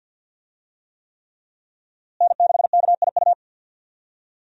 Morse code, heard as a keyed tone, about 51 wpm, 700 Hz. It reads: N6KIU
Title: USA Call Signs - 5 Characters - Repeated 3x - 50wpm